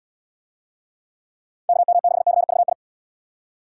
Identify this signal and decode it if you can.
Morse code, heard as a keyed tone, about 68 wpm, 700 Hz. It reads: KM6CVI